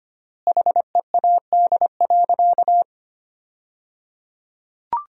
Morse code, heard as a keyed tone, about 25 wpm, 700 Hz. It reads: HEAD.
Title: 3x Word-Spacing - Sentences from Top 300 Words - 25wpm